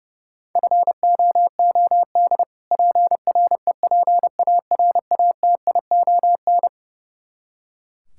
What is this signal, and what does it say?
Morse code, heard as a keyed tone, about 29 wpm, 700 Hz. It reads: FOOD PREPARATION
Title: Practice QSO - Exchange 3 - More Info - 30wpm